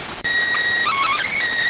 SSTV (slow-scan television), PD 290.